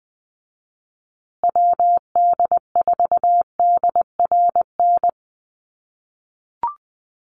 Morse code, heard as a keyed tone, about 20 wpm, 700 Hz. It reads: WD4DRN
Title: USA Call Signs Repeated 2x 20wpm